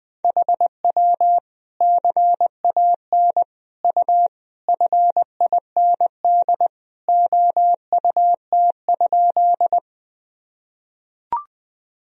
Morse code, heard as a keyed tone, about 20 wpm, 700 Hz. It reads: HW CAN U FIND OUT?
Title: Sentences from Top 200 Words and Abbreviations 20wpm